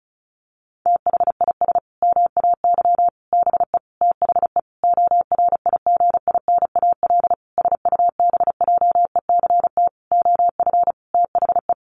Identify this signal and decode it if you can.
Morse code, heard as a keyed tone, about 34 wpm, 700 Hz. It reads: THIS MAY BE THE ORIGINAL SUBJECT OF THE
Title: Sentences from Top 900 Words 35wpm